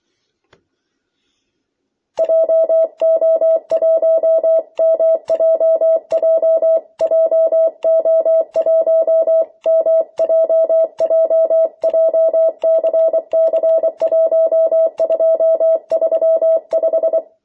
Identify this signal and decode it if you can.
Morse code, heard as a keyed tone, about 24 wpm, 650 Hz. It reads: JO1MJJ JO1MJJ JCC1235
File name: jh8cbh / 2021cw